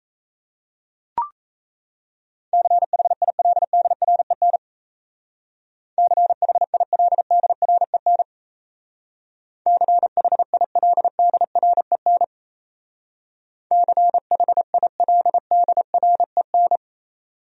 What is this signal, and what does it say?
Morse code, heard as a keyed tone, about 34 wpm, 700 Hz. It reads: CHILDREN CHILDREN CHILDREN CHILDREN
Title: Speed-Racing Top 500 Words 28wpm